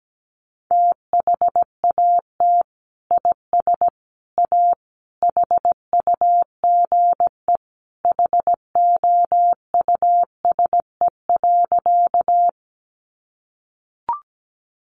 Morse code, heard as a keyed tone, about 17 wpm, 700 Hz. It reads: THAT IS A HUGE HOUSE.
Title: Sentences from Top 1100 Words 17wpm